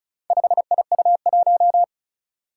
Morse code, heard as a keyed tone, about 36 wpm, 700 Hz. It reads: 5IU1
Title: Four Challenging Letter-Number Combinations 35wpm